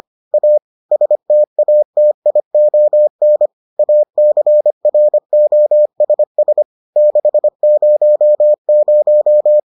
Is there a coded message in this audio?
Yes: Morse code from 0.3 to 9.6 s, heard as a keyed tone, about 25 wpm, 600 Hz. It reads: A STATION ACROSS 600